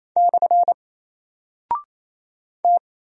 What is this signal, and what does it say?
Morse code, heard as a keyed tone, about 28 wpm, 700 Hz. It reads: / T
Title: Single Character (Letter:Number:Punctuation:Prosign) 28wpm